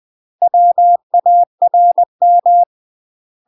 Morse code, heard as a keyed tone, about 20 wpm, 700 Hz. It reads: WARM
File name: UK MASTER QSOs TRAINING / 20WPM / UK MASTER QSOs 0046